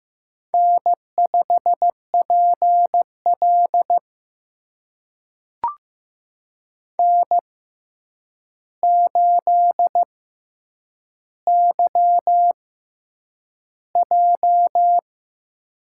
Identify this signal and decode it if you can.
Morse code, heard as a keyed tone, about 15 wpm, spaced as for 8 wpm, 700 Hz. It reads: N5PL N8YJ
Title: USA Call Signs - 4 Characters - 6x Character Spacing - 15wpm